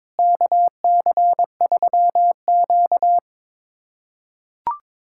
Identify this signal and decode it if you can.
Morse code, heard as a keyed tone, about 22 wpm, 700 Hz. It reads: KC3Q
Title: USA Callsign DE Callsign - 4 Characters - 22wpm